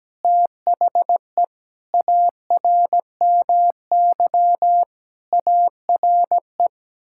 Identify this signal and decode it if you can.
Morse code, heard as a keyed tone, about 17 wpm, 700 Hz. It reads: THE ARMY ARE
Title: Sentences from Top 1000 Words - Review Part 6 - 17wpm